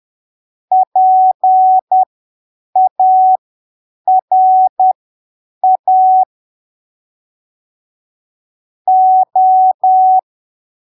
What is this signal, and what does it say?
Morse code, heard as a keyed tone, about 10 wpm, spaced as for 7 wpm, 750 Hz. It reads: PARA O
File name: arquivos / texto / texto 07 10ppm